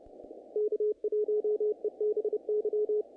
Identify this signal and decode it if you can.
Morse code, heard as a keyed tone, about 30 wpm, 420 Hz, with radio noise. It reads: K1EBY
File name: contests / naqp-cw-k5tr-op-k5pi- -2000